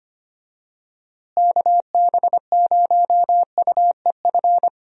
Morse code, heard as a keyed tone, about 25 wpm, 700 Hz. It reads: KB0UEF
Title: USA Call Signs - 6 Characters - Repeated 2x - 25wpm